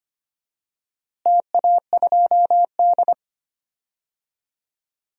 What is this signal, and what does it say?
Morse code, heard as a keyed tone, about 25 wpm, 700 Hz. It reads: TA2D
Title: DXCC-100 Call Signs - 4 Characters - Repeated 2x - 25wpm